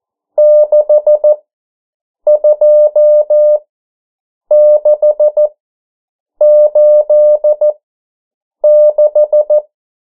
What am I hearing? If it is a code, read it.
Morse code, heard as a keyed tone, about 14 wpm, spaced as for 7 wpm, 600 Hz. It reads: 62686
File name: morse mp3 / 09 urok 8